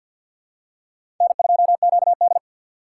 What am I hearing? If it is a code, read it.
Morse code, heard as a keyed tone, about 46 wpm, 700 Hz. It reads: N2QD
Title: USA Call Signs - 4 Characters - 2x Character Spacing - 50wpm